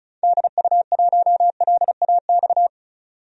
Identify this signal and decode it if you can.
Morse code, heard as a keyed tone, about 36 wpm, 700 Hz. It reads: DU1LAX
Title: DXCC-100 Call Signs - 6 Characters - 35wpm